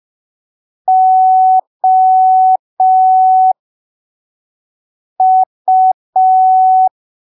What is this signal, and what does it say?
Morse code, heard as a keyed tone, about 5 wpm, spaced as for 3 wpm, 750 Hz. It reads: OU
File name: arquivos / texto / texto 11 05ppm